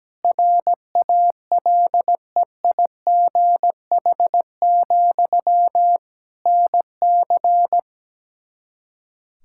Morse code, heard as a keyed tone, about 17 wpm, 700 Hz. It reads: RALEIGH, NC
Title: Practice QSO - Exchange 1 - Beginning - 17wpm